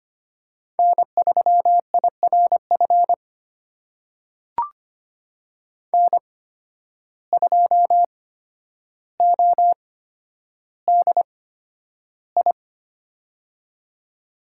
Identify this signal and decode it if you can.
Morse code, heard as a keyed tone, about 25 wpm, spaced as for 11 wpm, 700 Hz. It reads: N3IRF N2ODI
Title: USA Call Signs - 5 Characters - 8x Character Spacing - 25wpm